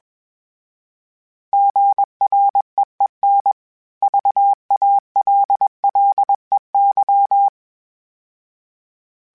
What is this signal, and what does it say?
Morse code, heard as a keyed tone, about 21 wpm, 800 Hz. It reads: GREEN VALLEY
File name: fromSMs / FromSMs mp3s / Cities@20